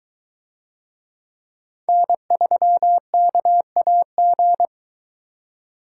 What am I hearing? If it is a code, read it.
Morse code, heard as a keyed tone, about 23 wpm, 700 Hz. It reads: N3KAG